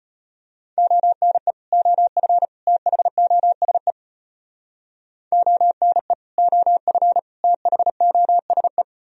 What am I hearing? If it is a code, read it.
Morse code, heard as a keyed tone, about 30 wpm, 700 Hz. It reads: ONE OF THOSE ONE OF THOSE
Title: Speed-Racing - Sets of 3 Words 25wpm